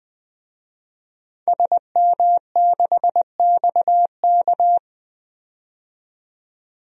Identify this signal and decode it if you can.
Morse code, heard as a keyed tone, about 20 wpm, 700 Hz. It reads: SM6XK
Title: DXCC-100 Call Signs - 5 Characters - 20wpm